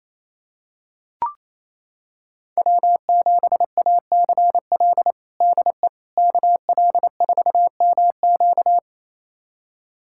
Morse code, heard as a keyed tone, about 28 wpm, 700 Hz. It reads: W7ACL DE KL4MQ
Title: USA Callsign DE Callsign - 5 Characters - 28wpm